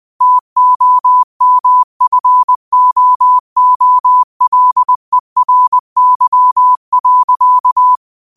Morse code, heard as a keyed tone, about 20 wpm, 1000 Hz. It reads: TOMFOOLERY.
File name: qotdfiles / 20wpm